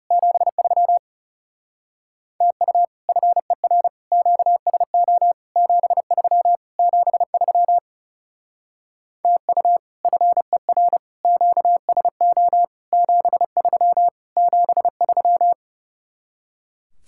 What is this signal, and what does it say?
Morse code, heard as a keyed tone, about 39 wpm, 700 Hz. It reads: 73 TU FER QSO 73 73 TU FER QSO 73 73